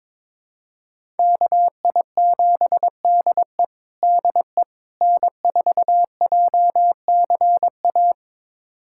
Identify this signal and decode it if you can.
Morse code, heard as a keyed tone, about 22 wpm, 700 Hz. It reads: KI7DE DE N4JCA
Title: USA Callsign DE Callsign - 5 Characters - 22wpm